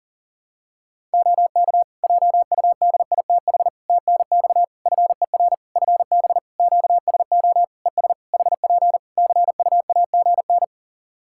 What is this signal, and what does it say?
Morse code, heard as a keyed tone, about 37 wpm, 700 Hz. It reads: OK JUDITH TNX FER FB QSO ES HP CUAGN
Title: Practice QSO - Exchange 4 - Ending - 40wpm